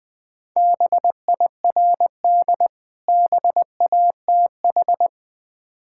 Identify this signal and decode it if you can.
Morse code, heard as a keyed tone, about 20 wpm, 690 Hz. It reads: BIRD BATH
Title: Difficult Words with Hints - 20wpm